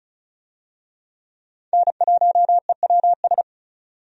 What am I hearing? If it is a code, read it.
Morse code, heard as a keyed tone, about 34 wpm, 700 Hz. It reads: N1EWS
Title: USA Call Signs - 5 Characters - 35wpm